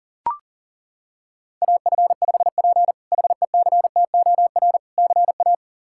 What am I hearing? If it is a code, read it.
Morse code, heard as a keyed tone, about 40 wpm, 700 Hz. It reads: AF5P HECTOR CA
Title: K1USN SST QSO Practice - 40wpm